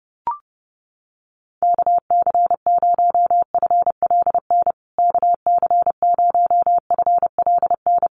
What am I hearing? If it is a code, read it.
Morse code, heard as a keyed tone, about 28 wpm, 700 Hz. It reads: KC0FLN KC0FLN
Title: USA Call Signs Repeated 3x 30wpm